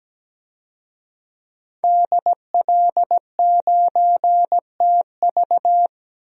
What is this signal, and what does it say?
Morse code, heard as a keyed tone, about 17 wpm, 700 Hz. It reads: DL9TV
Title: DXCC-100 Call Signs - 5 Characters - 17wpm